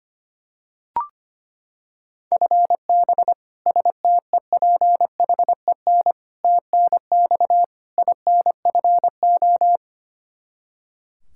Morse code, heard as a keyed tone, about 25 wpm, 700 Hz. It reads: FB STEPHEN TNX INFO
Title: Practice QSO - Exchange 3 - More Info - 25wpm